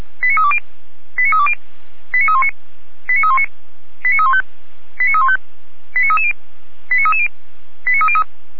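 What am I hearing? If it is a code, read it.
Five-tone selective call (ZVEI), heard as quick runs of five tones, tones of about 70 ms. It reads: 89320 89320 89319 89319 89315 89315 89330 89330 89303
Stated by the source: SDR reception of a ZVEI five-tone selective call.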